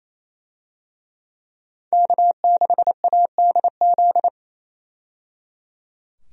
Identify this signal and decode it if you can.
Morse code, heard as a keyed tone, about 28 wpm, 700 Hz. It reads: K6ADZ
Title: USA Call Signs - 5 Characters - 28wpm